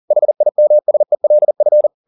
Morse code, heard as a keyed tone, about 40 wpm, 600 Hz. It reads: HIMSELF
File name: the old man and the sea 2 40wpm